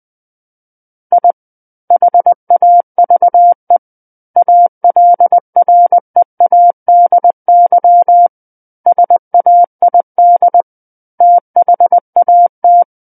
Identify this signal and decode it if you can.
Morse code, heard as a keyed tone, about 20 wpm, 700 Hz. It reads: I HAVE ALREADY SAID THAT
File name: 20 WPM